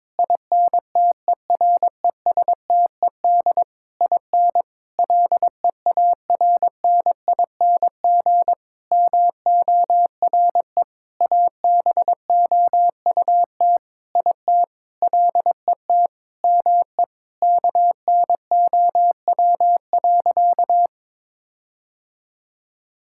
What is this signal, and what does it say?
Morse code, heard as a keyed tone, about 22 wpm, 700 Hz. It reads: INTERESTED IN LEARNING MORE ABOUT IT LET ME KNOW.
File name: practice / 22wpm / Sentences with 8-Word Phrases - 22wpm